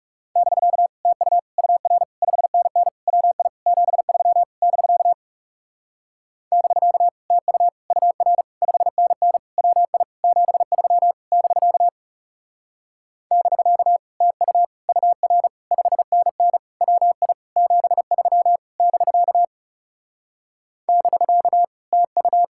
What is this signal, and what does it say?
Morse code, heard as a keyed tone, about 37 wpm, 700 Hz. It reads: <BK> TU UR 5NN WI 73 <BK> <BK> TU UR 5NN WI 73 <BK> <BK> TU UR 5NN WI 73 <BK> <BK> TU
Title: Speed-Racing Common QSO Phrases - 30wpm